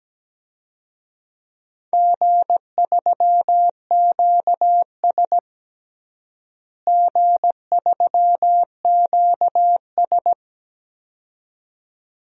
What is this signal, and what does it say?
Morse code, heard as a keyed tone, about 17 wpm, 700 Hz. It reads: G3QS G3QS
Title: DXCC-100 Call Signs - 4 Characters - Repeated 2x - 17wpm